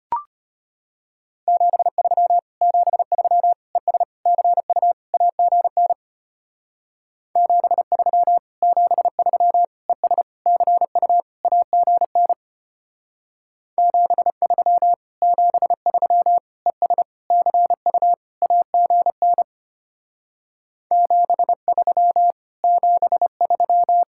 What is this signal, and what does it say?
Morse code, heard as a keyed tone, about 31 wpm, 700 Hz. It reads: 73 73 ES CU AGN 73 73 ES CU AGN 73 73 ES CU AGN 73 73
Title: Speed-Racing Common QSO Phrases - 25wpm